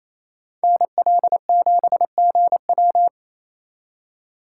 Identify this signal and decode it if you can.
Morse code, heard as a keyed tone, about 35 wpm, 700 Hz. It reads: NL7GW